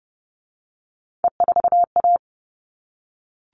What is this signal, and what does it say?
Morse code, heard as a keyed tone, about 30 wpm, 700 Hz. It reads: E4A
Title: All Cut Numbers - 3-Digit 30wpm